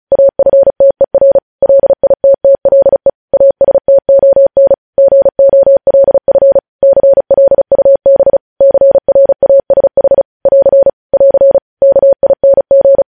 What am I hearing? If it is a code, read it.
Morse code, heard as a keyed tone, about 36 wpm, 550 Hz. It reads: AFTER LITTLE ASTON GOLF CLUB CRASH <AR> <AR> KING
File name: podcast / News headlines-1x-35x35